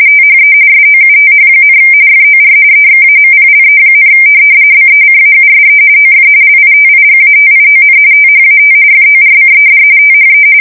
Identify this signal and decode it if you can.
RTTY, about 45.45 baud. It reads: THE SOUTH, FOR EXAMPLE, BECAUSE OF LITIGATION THAT FORCED THE
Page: 170 Shift 2125/2295